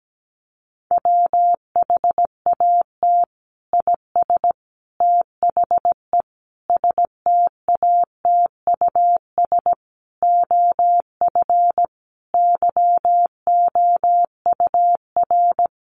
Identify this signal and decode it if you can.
Morse code, heard as a keyed tone, about 17 wpm, 700 Hz. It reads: WHAT IS THE STATUS OF YOUR
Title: Sentences from Top 1400 Words 17wpm